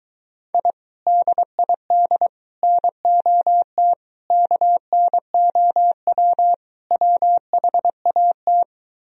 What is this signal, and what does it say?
Morse code, heard as a keyed tone, about 24 wpm, 700 Hz. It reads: I DID NOT KNOW WHAT